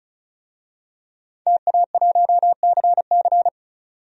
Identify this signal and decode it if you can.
Morse code, heard as a keyed tone, about 36 wpm, 700 Hz. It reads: TA1CC
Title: DXCC-100 Call Signs - 5 Characters - 35wpm